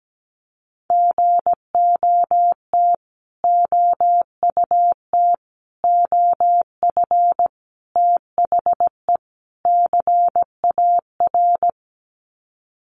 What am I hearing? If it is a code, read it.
Morse code, heard as a keyed tone, about 17 wpm, 700 Hz. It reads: GOT OUT OF THE CAR
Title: Sets of 5 Words Encore 17wpm